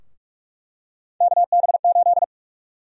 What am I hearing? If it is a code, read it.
Morse code, heard as a keyed tone, about 46 wpm, 700 Hz. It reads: KB8
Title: USA Call Sign Prefixes - Repeated 2x - 45wpm